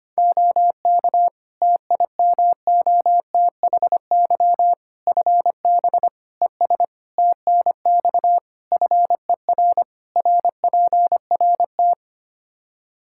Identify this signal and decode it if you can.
Morse code, heard as a keyed tone, about 25 wpm, 700 Hz. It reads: OK TIMOTHY FB ES TNX FER RPRT
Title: K1USN SST QSO Practice 25wpm